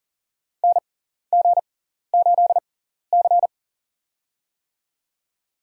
Morse code, heard as a keyed tone, about 40 wpm, spaced as for 14 wpm, 700 Hz. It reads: NG8C
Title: USA Call Signs - 4 Characters - 6x Character Spacing - 40wpm